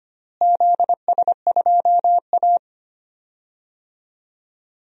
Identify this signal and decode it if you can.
Morse code, heard as a keyed tone, about 25 wpm, 700 Hz. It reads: ZS2A
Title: DXCC-100 Call Signs - 4 Characters - Repeated 2x - 25wpm